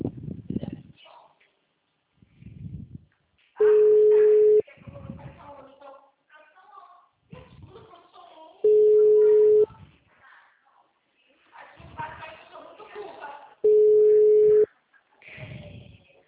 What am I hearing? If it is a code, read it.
Ringback tone (Europe, 425 Hz), with speech.
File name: out- -2009- - - .63403